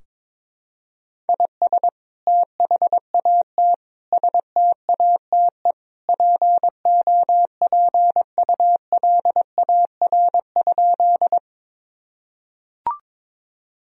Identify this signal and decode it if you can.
Morse code, heard as a keyed tone, about 22 wpm, 700 Hz. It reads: IS THAT STATE POPULAR?
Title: Sentences from Top 1000 Words - Review Part 3 - 22wpm